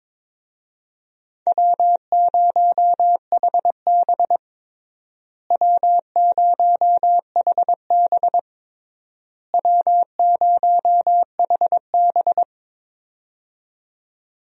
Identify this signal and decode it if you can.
Morse code, heard as a keyed tone, about 22 wpm, 700 Hz. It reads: W0HB W0HB W0HB
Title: USA Call Signs - 4 Characters - Repeated 3x - 22wpm